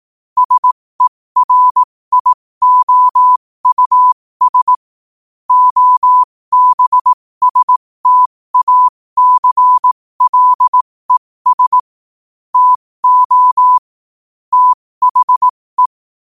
Morse code, heard as a keyed tone, about 18 wpm, spaced as for 15 wpm, 1000 Hz. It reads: SERIOUS OBSTACLES TO THE
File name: qotdfiles / 15wpm